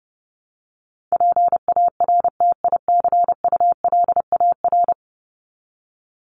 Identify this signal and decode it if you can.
Morse code, heard as a keyed tone, about 30 wpm, 700 Hz. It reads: PARTICULAR
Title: LY Suffix - Top 100 Words with Base Words 30wpm